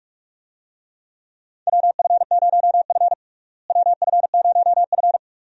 Morse code, heard as a keyed tone, about 42 wpm, 700 Hz. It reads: WF0F WF0F
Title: USA Call Signs - 4 Characters - Repeated 2x - 45wpm